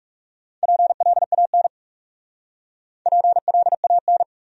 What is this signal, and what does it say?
Morse code, heard as a keyed tone, about 37 wpm, 700 Hz. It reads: PLAN PLAN
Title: Speed-Racing Top 500 Words 30wpm